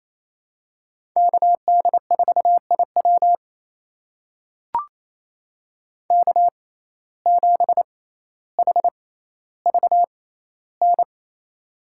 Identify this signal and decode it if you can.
Morse code, heard as a keyed tone, about 28 wpm, spaced as for 14 wpm, 700 Hz. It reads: KD4IW K7HVN
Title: USA Call Signs - 5 Characters - 6x Character Spacing - 28wpm